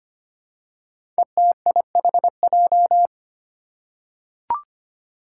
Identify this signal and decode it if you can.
Morse code, heard as a keyed tone, about 25 wpm, 700 Hz. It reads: ETIHJ
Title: Five Challenging Letter-Number Combinations 25wpm